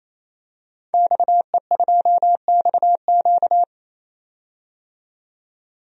Morse code, heard as a keyed tone, about 27 wpm, 700 Hz. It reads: XE2XQ
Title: DXCC-100 Call Signs - 5 Characters - 28wpm